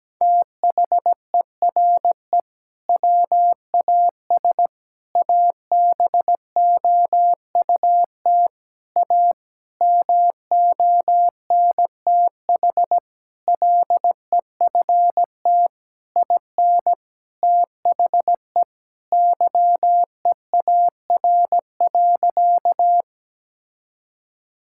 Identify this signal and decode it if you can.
Morse code, heard as a keyed tone, about 17 wpm, 700 Hz. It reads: THERE WAS ABOUT A MONTH LEFT IN THE YEAR.
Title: Sentences from Top 1000 Words - Review Part 5 - 17wpm